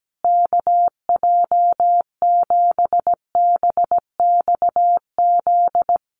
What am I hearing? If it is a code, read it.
Morse code, heard as a keyed tone, about 17 wpm, 700 Hz. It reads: KJ7BXZ
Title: USA Call Signs Repeated 3x 17wpm